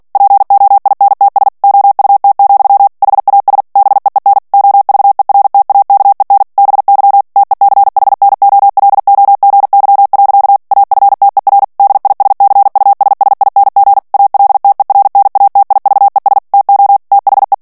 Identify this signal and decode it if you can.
Morse code, heard as a keyed tone, about 48 wpm, 800 Hz. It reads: POINTS OUT, HAS BEEN OVERTAKEN BY TECHNOLOGY. AFTER DISCUSSING ALTERNATIVES TO THE
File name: ANGLAIS texte clair 3 vitesse 45 mots minute